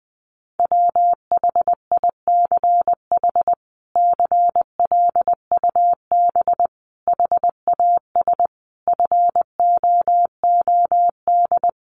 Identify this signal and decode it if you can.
Morse code, heard as a keyed tone, about 20 wpm, 700 Hz. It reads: WHICH CLUB HAS FOOD
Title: Sentences from Top 1200 Words 20wpm